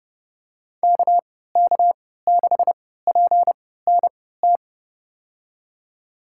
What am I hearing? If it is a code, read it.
Morse code, heard as a keyed tone, about 30 wpm, spaced as for 17 wpm, 700 Hz. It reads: KK6PNT
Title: USA Call Signs - 3x Character Spacing - 30wpm